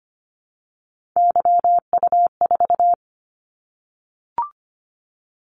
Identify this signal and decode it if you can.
Morse code, heard as a keyed tone, about 25 wpm, 700 Hz. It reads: YU4
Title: Callsign Prefixes for Top 25 DXCC - 25wpm